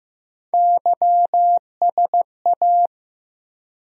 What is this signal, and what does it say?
Morse code, heard as a keyed tone, about 15 wpm, 700 Hz. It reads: YSA
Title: USA Call Sign Postfixes - 3 Letters 15wpm